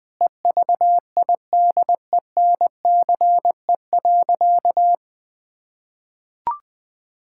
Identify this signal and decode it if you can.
Morse code, heard as a keyed tone, about 20 wpm, 700 Hz. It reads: EVIDENCE.
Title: Sentences with 4-Word Phrases - 20wpm